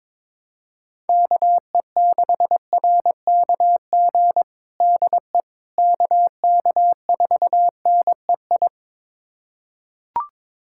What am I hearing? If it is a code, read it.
Morse code, heard as a keyed tone, about 22 wpm, 700 Hz. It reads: KE6RKG DE KK4NEI
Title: USA Callsign DE Callsign 22wpm